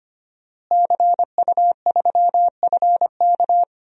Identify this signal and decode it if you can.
Morse code, heard as a keyed tone, about 25 wpm, 700 Hz. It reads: CU3FK
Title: DXCC-100 Call Signs - 5 Characters - Repeated 3x - 25wpm